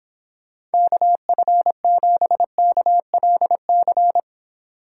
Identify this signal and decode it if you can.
Morse code, heard as a keyed tone, about 26 wpm, 700 Hz. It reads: KF7KLC